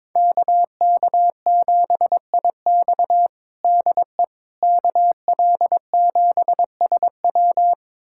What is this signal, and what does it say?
Morse code, heard as a keyed tone, about 22 wpm, 700 Hz. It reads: KK7IX DE KL7SW
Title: USA Callsign DE Callsign - 5 Characters - 22wpm